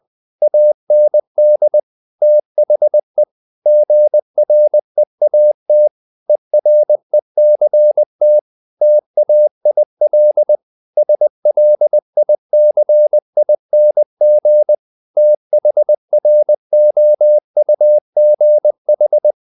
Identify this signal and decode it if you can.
Morse code, heard as a keyed tone, about 20 wpm, 600 Hz. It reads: AND THE GREAT ERECT TAIL SLICING THROUGH
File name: the old man and the sea / the old man and the sea part 1 20wpm